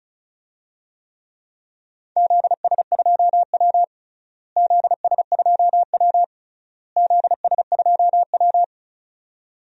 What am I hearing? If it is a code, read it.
Morse code, heard as a keyed tone, about 36 wpm, 700 Hz. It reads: ZS2W ZS2W ZS2W
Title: DXCC-100 Call Signs - 4 Characters - Repeated 3x - 35wpm